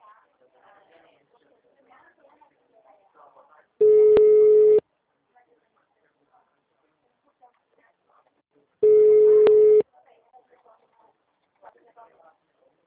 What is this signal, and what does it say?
Ringback tone (Europe, 425 Hz), with speech.